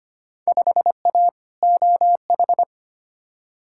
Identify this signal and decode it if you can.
Morse code, heard as a keyed tone, about 25 wpm, 700 Hz. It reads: 5A OH
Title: Field Day Exchanges Encore - 25wpm